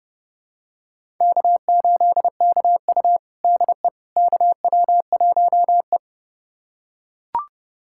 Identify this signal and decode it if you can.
Morse code, heard as a keyed tone, about 31 wpm, 700 Hz. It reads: K8KU DE KW1E
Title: USA Callsign DE Callsign - 4 Characters - 30wpm